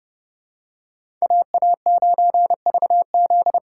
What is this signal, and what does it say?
Morse code, heard as a keyed tone, about 30 wpm, 700 Hz. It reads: AA9VZ
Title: USA Call Signs - 8x Character Spacing - 30wpm